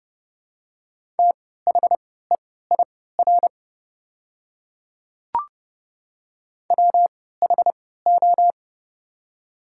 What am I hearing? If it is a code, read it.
Morse code, heard as a keyed tone, about 29 wpm, spaced as for 17 wpm, 700 Hz. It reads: THEIR WHO
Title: 3x Character Spacing - Top 100 Words - 30wpm